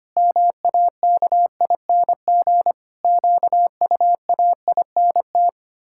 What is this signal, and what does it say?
Morse code, heard as a keyed tone, about 25 wpm, 700 Hz. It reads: MAKING QUAINT
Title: Pangram Challenge v2 - 25wpm